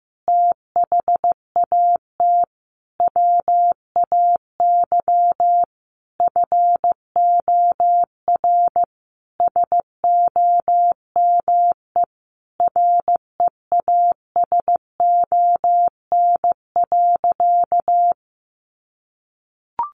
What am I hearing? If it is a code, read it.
Morse code, heard as a keyed tone, about 15 wpm, 700 Hz. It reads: THAT WAY FOR SOME REASON.
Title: Sentences from Top 700 Words 15wpm